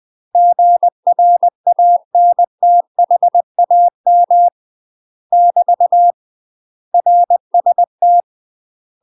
Morse code, heard as a keyed tone, about 20 wpm, 700 Hz. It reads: GRANTHAM <BT> RST
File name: UK MASTER QSOs TRAINING / 20WPM / UK MASTER QSOs 0025